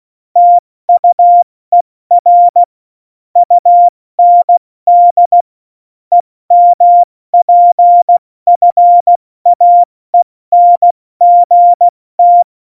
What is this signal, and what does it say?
Morse code, heard as a keyed tone, about 16 wpm, 700 Hz. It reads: TUER UND EMPFAENGT
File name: Morsekurs / Witze / Witz 21 Juli 2020, 70 F80